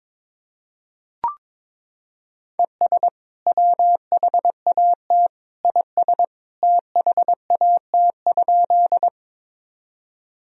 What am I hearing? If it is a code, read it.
Morse code, heard as a keyed tone, about 21 wpm, 700 Hz. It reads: ES WHAT IS THAT?
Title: Sentences from Top 100 Words and Abbreviations 22wpm